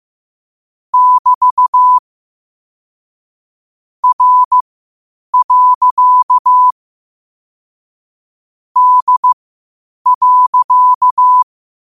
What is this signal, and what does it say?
Morse code, heard as a keyed tone, about 15 wpm, spaced as for 8 wpm, 1000 Hz. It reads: <BT> R. D.
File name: qotdfiles / 8wpm